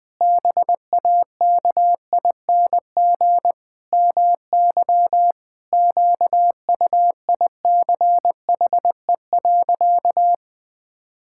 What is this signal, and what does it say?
Morse code, heard as a keyed tone, about 20 wpm, 700 Hz. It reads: BAKING MY QUICHE.
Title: Pangram Challenge v2 - 20wpm